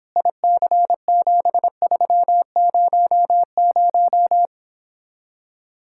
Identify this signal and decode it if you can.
Morse code, heard as a keyed tone, about 26 wpm, 700 Hz. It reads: IC7300
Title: Speed-Racing Common QSO Phrases Encore - 22wpm